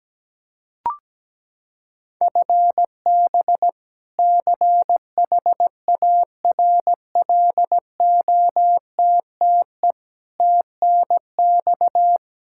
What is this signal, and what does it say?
Morse code, heard as a keyed tone, about 17 wpm, 700 Hz. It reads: FB CHARLOTTE TNX
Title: Practice QSO - Exchange 3 - More Info - 17wpm